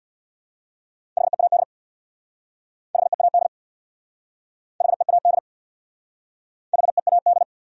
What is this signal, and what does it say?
Morse code, heard as a keyed tone, about 62 wpm, 700 Hz. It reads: HEAD HEAD HEAD HEAD
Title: Speed-Racing Top 500 Words 50wpm